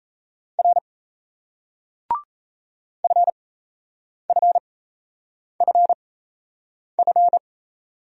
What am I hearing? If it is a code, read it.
Morse code, heard as a keyed tone, about 37 wpm, spaced as for 8 wpm, 700 Hz. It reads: R FFFF